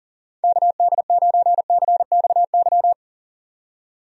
Morse code, heard as a keyed tone, about 37 wpm, 700 Hz. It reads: KD9CXY
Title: USA Call Signs - 6 Characters - 40wpm